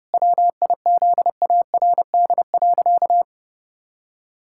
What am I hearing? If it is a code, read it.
Morse code, heard as a keyed tone, about 32 wpm, 700 Hz. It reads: WIZARD.
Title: Pangram Challenge v2 - 30wpm